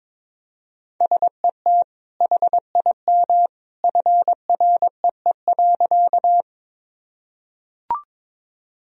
Morse code, heard as a keyed tone, about 22 wpm, 700 Hz. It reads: SET HIM FREE.
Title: Sentences from Top 1000 Words - Review Part 1 - 22wpm